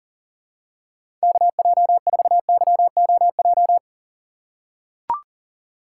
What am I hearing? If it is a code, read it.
Morse code, heard as a keyed tone, about 40 wpm, 700 Hz. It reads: KJ4YOJ
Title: USA Call Signs - 6 Characters - 6x Character Spacing - 40wpm